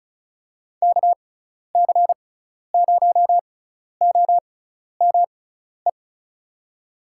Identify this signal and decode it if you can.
Morse code, heard as a keyed tone, about 34 wpm, spaced as for 12 wpm, 700 Hz. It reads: KC0OME
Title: USA Call Signs - 6 Characters - 6x Character Spacing - 35wpm